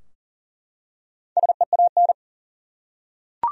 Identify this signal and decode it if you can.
Morse code, heard as a keyed tone, about 40 wpm, 700 Hz. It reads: SEAN
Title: USA Common Names - 40wpm